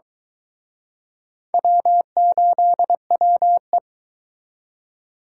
Morse code, heard as a keyed tone, about 23 wpm, 700 Hz. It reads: W8WE